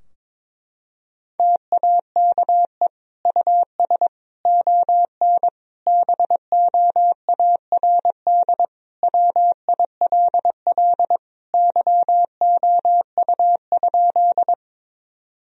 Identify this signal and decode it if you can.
Morse code, heard as a keyed tone, about 22 wpm, 700 Hz. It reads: TAKE US ON BOARD WILL YOU?
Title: Sentences from Top 1000 Words - Review Part 4 - 22wpm